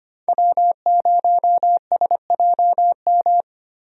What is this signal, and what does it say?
Morse code, heard as a keyed tone, about 25 wpm, 700 Hz. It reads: W0SJM
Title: USA Call Signs - 5 Characters - Repeated 3x - 25wpm